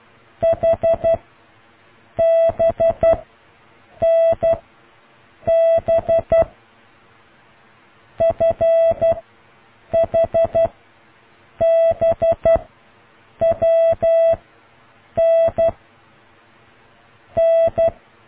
Morse code, heard as a keyed tone, about 12 wpm, 660 Hz. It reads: H B N B F H B W N N